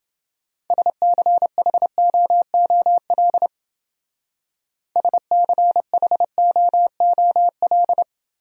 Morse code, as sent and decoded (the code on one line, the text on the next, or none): ... -.-. .... --- --- .-.. / ... -.-. .... --- --- .-..
SCHOOL SCHOOL